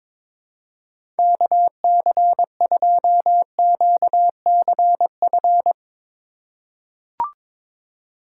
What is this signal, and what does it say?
Morse code, heard as a keyed tone, about 22 wpm, 700 Hz. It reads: KC2QCF
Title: USA Call Signs - 6 Characters - 2x Character Spacing - 22wpm